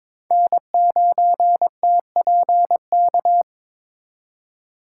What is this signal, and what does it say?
Morse code, heard as a keyed tone, about 22 wpm, 700 Hz. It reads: N9TPK
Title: USA Call Signs - 5 Characters - 4x Character Spacing - 22wpm